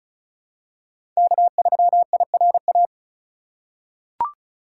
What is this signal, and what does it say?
Morse code, heard as a keyed tone, about 36 wpm, 700 Hz. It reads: K3IRA
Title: USA Call Signs - 5 Characters - 2x Character Spacing - 35wpm